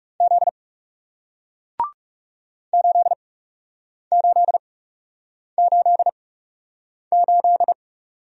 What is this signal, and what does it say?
Morse code, heard as a keyed tone, about 40 wpm, spaced as for 8 wpm, 700 Hz. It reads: Z 8888